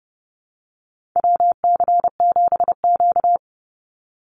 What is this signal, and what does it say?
Morse code, heard as a keyed tone, about 30 wpm, 700 Hz. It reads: WC7Q
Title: Call Signs Repeated 2x - 30wpm